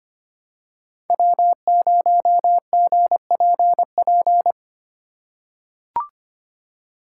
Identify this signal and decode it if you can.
Morse code, heard as a keyed tone, about 25 wpm, 700 Hz. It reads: W0GPP
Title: USA Call Signs - 5 Characters - 4x Character Spacing - 25wpm